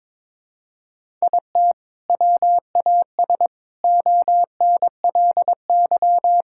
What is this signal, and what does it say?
Morse code, heard as a keyed tone, about 22 wpm, 700 Hz. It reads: IT WAS ONLY
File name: Sets of 9 Words - 22wpm